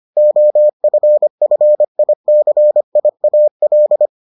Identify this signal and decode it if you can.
Morse code, heard as a keyed tone, about 25 wpm, 600 Hz. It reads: OFFICIAL